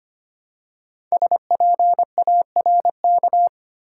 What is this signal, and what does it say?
Morse code, heard as a keyed tone, about 25 wpm, 700 Hz. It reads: SPARK
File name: 5 Letter Words - First 3 Letters Given - 25wpm